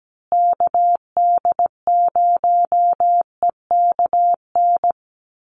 Morse code, heard as a keyed tone, about 17 wpm, 700 Hz. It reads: KD0EKN
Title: USA Call Signs Repeated 3x 17wpm